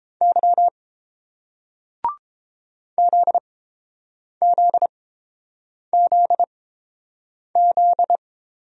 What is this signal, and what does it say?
Morse code, heard as a keyed tone, about 30 wpm, spaced as for 7 wpm, 690 Hz. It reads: Y ZZZZ